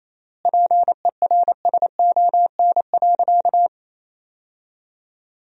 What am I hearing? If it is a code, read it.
Morse code, heard as a keyed tone, about 28 wpm, 700 Hz. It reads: PERSON.
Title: 3x Word-Spacing - Sentences from Top 500 Words - 28wpm